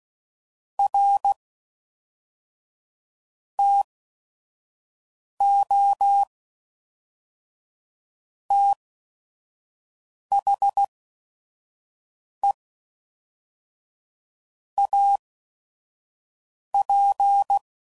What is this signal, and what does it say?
Morse code, heard as a keyed tone, about 16 wpm, 800 Hz. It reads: R T O T H E A P